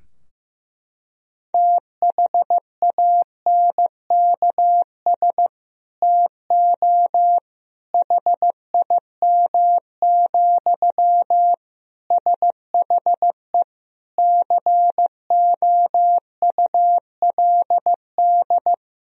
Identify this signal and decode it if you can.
Morse code, heard as a keyed tone, about 15 wpm, 700 Hz. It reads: THANKS TO HIM, SHE COULD
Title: Sentences from Top 1000 Words - Review Part 3 - 15wpm